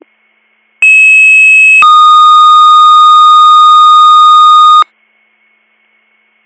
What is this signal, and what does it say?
Two-tone page, heard as a one-second tone, then a three-second tone, about 2600 then 1230 Hz. It is heard from 0.8 to 4.8 s.